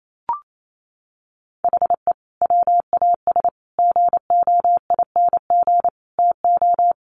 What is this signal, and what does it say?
Morse code, heard as a keyed tone, about 28 wpm, 700 Hz. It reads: HE WAS GOING TO
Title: Sets of 4 Words 28wpm